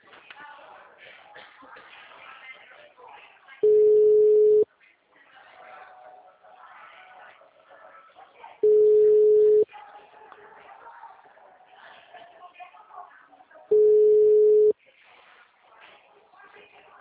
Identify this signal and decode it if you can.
Ringback tone (Europe, 425 Hz), with speech.
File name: out- -2011- - - .34772